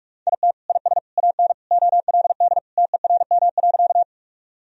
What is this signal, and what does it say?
Morse code, heard as a keyed tone, about 46 wpm, 700 Hz. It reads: IT IS AN OLD TERM.
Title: Sentences from Top 1000 Words - Review Part 4 - 45wpm